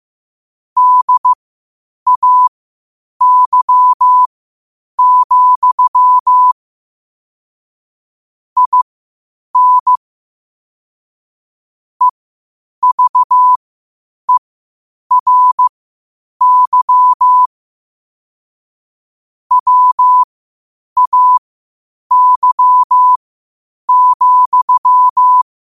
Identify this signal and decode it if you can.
Morse code, heard as a keyed tone, about 15 wpm, spaced as for 8 wpm, 1000 Hz. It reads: DAY, IN EVERY WAY,
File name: qotdfiles / 8wpm